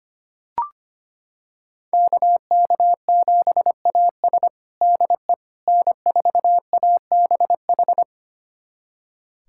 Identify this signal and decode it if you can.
Morse code, heard as a keyed tone, about 25 wpm, 700 Hz. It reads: KK7AS DE N4ABH
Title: USA Callsign DE Callsign - 5 Characters - 25wpm